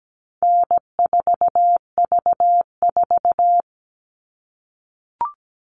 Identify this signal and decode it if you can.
Morse code, heard as a keyed tone, about 17 wpm, 700 Hz. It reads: N4V4
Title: All Cut Numbers - 4-Digit 17wpm